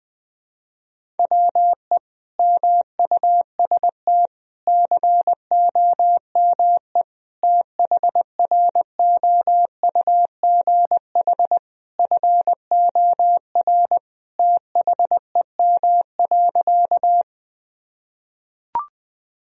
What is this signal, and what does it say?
Morse code, heard as a keyed tone, about 20 wpm, 700 Hz. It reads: WE MUST COME THROUGH FOR THEM.
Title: Sentences from Top 1000 Words - Review Part 1 - 20wpm